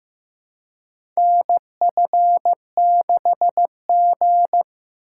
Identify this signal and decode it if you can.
Morse code, heard as a keyed tone, about 15 wpm, 700 Hz. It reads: NF6G
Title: USA Call Signs - 3x Character Spacing - 15wpm